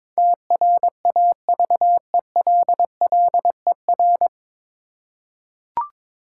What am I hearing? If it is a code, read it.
Morse code, heard as a keyed tone, about 22 wpm, 700 Hz. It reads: TRAVELLER
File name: 9 Letter Words - First 3 Letters Given - 22wpm